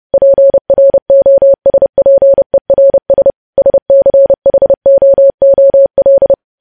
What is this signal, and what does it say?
Morse code, heard as a keyed tone, about 30 wpm, 550 Hz. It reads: PROSPERS SCHOOL
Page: news headlines in morse code at 30 wpm